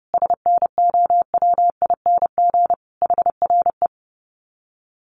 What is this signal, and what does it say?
Morse code, heard as a keyed tone, about 30 wpm, 700 Hz. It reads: SNOWING HRE
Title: Common QSO Phrases 30wpm